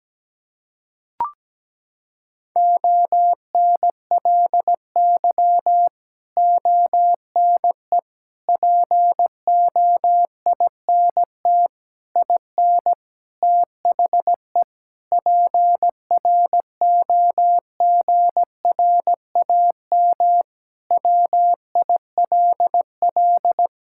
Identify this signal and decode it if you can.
Morse code, heard as a keyed tone, about 17 wpm, 700 Hz. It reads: ONLY ONE POINT IN THE PROGRAM WILL
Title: Sentences from Top 1000 Words - Review Part 5 - 17wpm